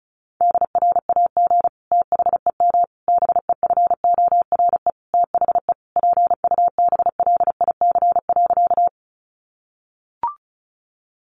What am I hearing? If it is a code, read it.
Morse code, heard as a keyed tone, about 34 wpm, 700 Hz. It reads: DRAG THEM BEFORE THE PUBLIC.
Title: Sentences from Top 1500 Words 35wpm